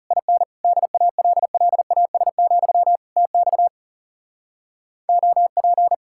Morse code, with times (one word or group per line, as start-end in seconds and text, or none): 0.1-0.4 IN
0.6-3.0 DALLAS,
3.2-3.7 TX
5.1-5.9 OP